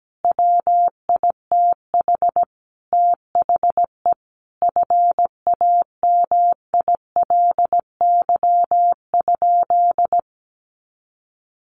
Morse code, heard as a keyed tone, about 17 wpm, 700 Hz. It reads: WITH THE FAMILY?
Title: Sentences from Top 1100 Words 17wpm